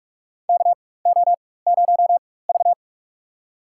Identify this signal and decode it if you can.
Morse code, heard as a keyed tone, about 41 wpm, spaced as for 20 wpm, 700 Hz. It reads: KO0V
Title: USA Call Signs - 4 Characters - 4x Character Spacing - 45wpm